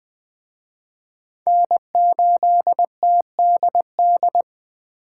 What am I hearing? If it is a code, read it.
Morse code, heard as a keyed tone, about 24 wpm, 700 Hz. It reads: N8TDD
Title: Speed Racing - USA 1x3 Call Signs 20wpm